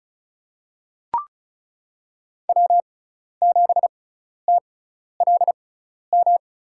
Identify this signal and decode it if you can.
Morse code, heard as a keyed tone, about 36 wpm, spaced as for 12 wpm, 700 Hz. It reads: W7TLM
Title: USA Call Signs - 5 Characters - 6x Character Spacing - 35wpm